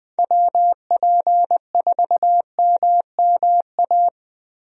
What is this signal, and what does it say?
Morse code, heard as a keyed tone, about 21 wpm, 700 Hz. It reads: WP4MMA
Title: USA Call Signs - 6 Characters - Repeated 2x - 20wpm